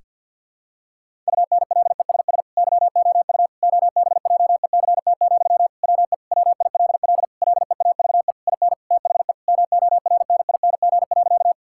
Morse code, heard as a keyed tone, about 52 wpm, 700 Hz. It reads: UNLESS YOU OBJECT, WE WILL LEAVE IN THE MORNING.
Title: Sentences from Top 1000 Words - Review Part 6 - 50wpm